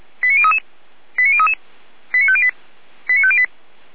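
Five-tone selective call (ZVEI), heard as quick runs of five tones, tones of about 70 ms. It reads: 89033 89033 89508 89508
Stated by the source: ZVEI five-tone selective call received on an SDR.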